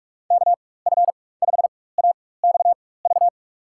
Morse code, heard as a keyed tone, about 46 wpm, spaced as for 21 wpm, 700 Hz. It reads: KF5AXV
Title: USA Call Signs - 6 Characters - 4x Character Spacing - 45wpm